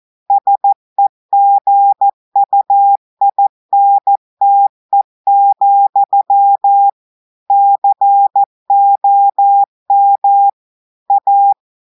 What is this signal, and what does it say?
Morse code, heard as a keyed tone, about 14 wpm, 800 Hz. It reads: SEGUINTE, COM A
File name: audios / portugais / portugais texte en clair 1 vitesse 14 mots minutes